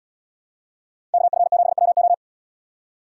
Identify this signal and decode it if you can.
Morse code, heard as a keyed tone, about 73 wpm, 700 Hz. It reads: KF7WO